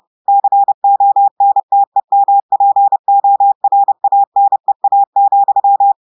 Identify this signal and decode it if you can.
Morse code, heard as a keyed tone, about 30 wpm, 800 Hz. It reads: CONTEMPORANEA,
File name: audios / portugais / portugais texte en clair 1 vitesse 30 mots minutes